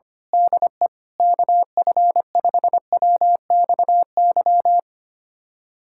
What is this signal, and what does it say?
Morse code, heard as a keyed tone, about 25 wpm, 700 Hz. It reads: DE KF5WXY
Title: USA Callsign DE Callsign 25wpm